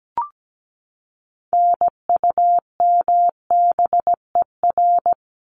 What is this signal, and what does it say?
Morse code, heard as a keyed tone, about 17 wpm, 700 Hz. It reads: NUMBER
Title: ES:S Suffix - Top 100 Words with Base Words 17wpm